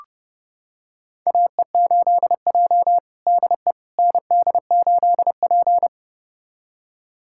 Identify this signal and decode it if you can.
Morse code, heard as a keyed tone, about 31 wpm, 700 Hz. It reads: AE8J DE ND8P
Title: USA Callsign DE Callsign - 4 Characters - 30wpm